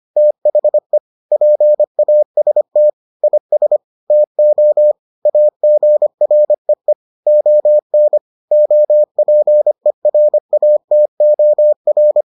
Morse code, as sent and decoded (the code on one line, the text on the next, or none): - .... / .--. .- ... - / .. ... / - --- / .- --. .-. . / --- -. / --- .--. .-. .- - --- .-.
THE PAST IS TO AGREE ON OPERATOR